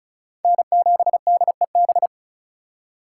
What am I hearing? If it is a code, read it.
Morse code, heard as a keyed tone, about 34 wpm, 700 Hz. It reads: N7DEB
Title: USA Call Signs - 5 Characters - 3x Character Spacing - 35wpm